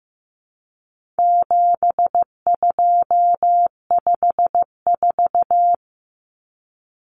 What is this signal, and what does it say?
Morse code, heard as a keyed tone, about 15 wpm, 700 Hz. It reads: 7254
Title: Common Cut Numbers - 4-Digit 15wpm